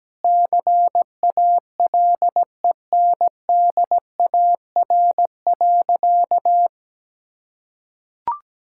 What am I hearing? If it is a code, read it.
Morse code, heard as a keyed tone, about 17 wpm, 700 Hz. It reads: CALENDAR.
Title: Sentences from Top 1600 Words 17wpm